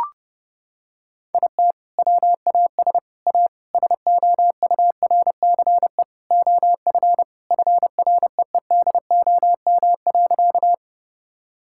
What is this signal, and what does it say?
Morse code, heard as a keyed tone, about 32 wpm, 700 Hz. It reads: IT WAS A SOURCE OF FREEDOM.
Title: Sentences from Top 1000 Words - Review Part 5 - 30wpm